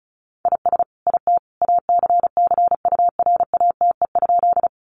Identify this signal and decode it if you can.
Morse code, heard as a keyed tone, about 34 wpm, 700 Hz. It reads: IS IT ACCURATE?
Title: Sentences from Top 1200 Words 35wpm